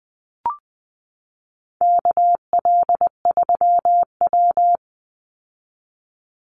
Morse code, heard as a keyed tone, about 20 wpm, 700 Hz. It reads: KL3W
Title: Call Signs Repeated 3x - 20wpm